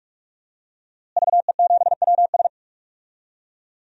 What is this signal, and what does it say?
Morse code, heard as a keyed tone, about 46 wpm, 700 Hz. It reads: VE7WS
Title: DXCC-100 Call Signs - 5 Characters - Repeated 2x - 45wpm